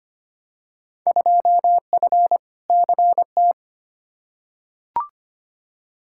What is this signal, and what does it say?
Morse code, heard as a keyed tone, about 25 wpm, 700 Hz. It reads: 2F CT
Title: Field Day Exchanges Encore - 25wpm